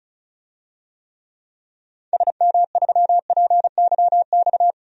Morse code, heard as a keyed tone, about 34 wpm, 700 Hz. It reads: SM3PYX
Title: DXCC-100 Call Signs - 6 Characters - 35wpm